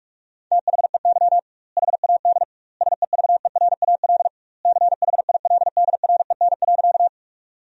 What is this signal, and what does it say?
Morse code, heard as a keyed tone, about 43 wpm, 700 Hz. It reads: THEY HAD SEVERAL CHILDREN.
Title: 2x Word-Spacing - Sentences from Top 400 Words - 45wpm